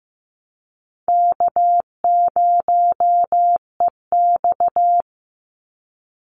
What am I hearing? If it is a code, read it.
Morse code, heard as a keyed tone, about 15 wpm, 700 Hz. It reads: K0EX
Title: Easy Call Signs Repeated 2x - 15wpm